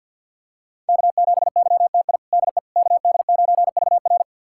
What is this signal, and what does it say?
Morse code, heard as a keyed tone, about 51 wpm, 700 Hz. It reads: K7YTI DE KD9VR
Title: USA Callsign DE Callsign - 5 Characters - 50wpm